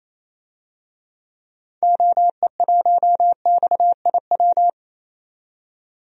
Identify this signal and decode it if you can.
Morse code, heard as a keyed tone, about 28 wpm, 700 Hz. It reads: OE1XIW
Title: DXCC-100 Call Signs - 6 Characters - 28wpm